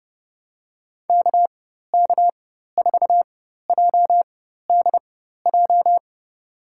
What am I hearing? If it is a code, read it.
Morse code, heard as a keyed tone, about 31 wpm, spaced as for 14 wpm, 700 Hz. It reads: KK4JDJ
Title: USA Call Signs - 4x Character Spacing - 30wpm